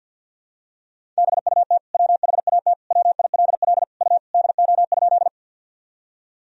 Morse code, heard as a keyed tone, about 49 wpm, 700 Hz. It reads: BUT WHAT WILL U DO?
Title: Sentences from Top 100 Words and Abbreviations 50wpm